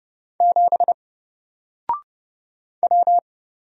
Morse code, heard as a keyed tone, about 31 wpm, 700 Hz. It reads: 7 W
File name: Single-Letter-Number-30wpm